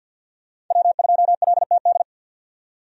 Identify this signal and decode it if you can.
Morse code, heard as a keyed tone, about 52 wpm, 700 Hz. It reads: W2LTD
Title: USA Call Signs - 5 Characters - 50wpm